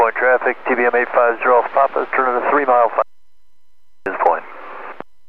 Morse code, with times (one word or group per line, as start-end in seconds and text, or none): none